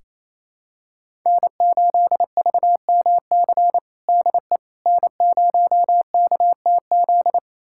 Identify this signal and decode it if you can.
Morse code, heard as a keyed tone, about 28 wpm, 700 Hz. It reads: N8VMC DE N0KTZ
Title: USA Callsign DE Callsign - 5 Characters - 28wpm